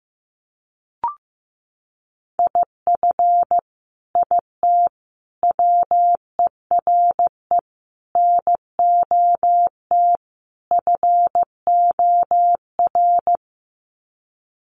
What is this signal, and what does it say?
Morse code, heard as a keyed tone, about 15 wpm, 700 Hz. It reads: IF IT WERE NOT FOR
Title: Sets of 5 Words 15wpm